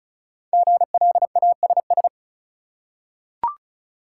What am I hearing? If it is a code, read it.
Morse code, heard as a keyed tone, about 36 wpm, 700 Hz. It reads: GLASS
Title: Five Letter Words - 35wpm